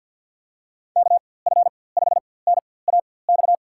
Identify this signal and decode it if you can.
Morse code, heard as a keyed tone, about 52 wpm, spaced as for 23 wpm, 700 Hz. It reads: KF5NAX
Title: USA Call Signs - 6 Characters - 4x Character Spacing - 50wpm